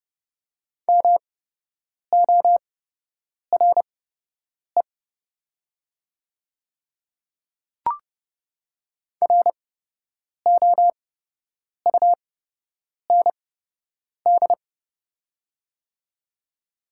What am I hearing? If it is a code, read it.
Morse code, heard as a keyed tone, about 30 wpm, spaced as for 8 wpm, 700 Hz. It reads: MORE ROUND
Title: 8x Character Spacing - Top 200 Words - 30wpm